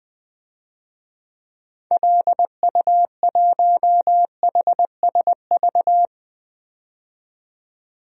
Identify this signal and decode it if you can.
Morse code, heard as a keyed tone, about 20 wpm, 700 Hz. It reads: LU1HSV
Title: DXCC-100 Call Signs - 6 Characters - 20wpm